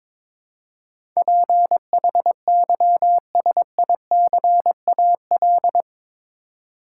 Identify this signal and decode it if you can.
Morse code, heard as a keyed tone, about 22 wpm, 700 Hz. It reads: PHYSICAL
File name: practice / 22wpm / Eight Letter Words - 22wpm